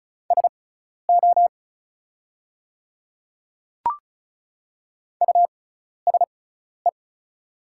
Morse code, heard as a keyed tone, about 34 wpm, spaced as for 12 wpm, 690 Hz. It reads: SO USE
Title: 6x Character Spacing - Top 200 Words - 35wpm